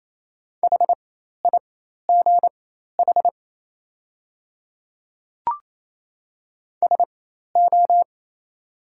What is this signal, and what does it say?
Morse code, heard as a keyed tone, about 28 wpm, spaced as for 13 wpm, 690 Hz. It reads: HIGH SO
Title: 4x Character Spacing - Top 200 Words - 28wpm